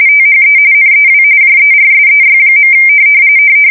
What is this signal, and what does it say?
RTTY, about 45.45 baud. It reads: THE NEW JERSEY.
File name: fsk170 / battleshipnj